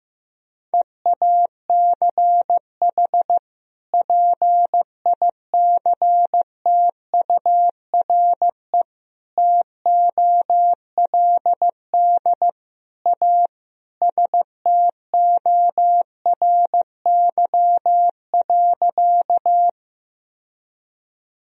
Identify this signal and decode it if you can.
Morse code, heard as a keyed tone, about 15 wpm, 700 Hz. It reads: EACH PICTURE TOLD A STORY.
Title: Sentences from Top 1000 Words - Review Part 1 - 15wpm